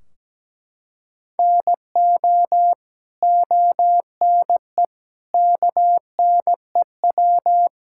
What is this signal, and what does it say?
Morse code, heard as a keyed tone, about 17 wpm, 700 Hz. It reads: NO ONE KNEW
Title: Sentences from Top 1600 Words 17wpm